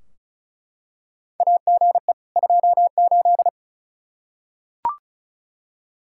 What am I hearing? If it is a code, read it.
Morse code, heard as a keyed tone, about 36 wpm, 700 Hz. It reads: AGE 28
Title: K1USN SST QSO Practice 35wpm